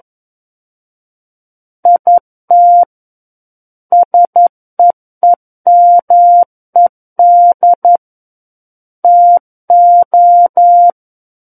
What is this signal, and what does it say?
Morse code, heard as a keyed tone, about 11 wpm, 700 Hz. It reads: IT SEEMED TO
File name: Book One / 02-Chapter 2 - 11 WPM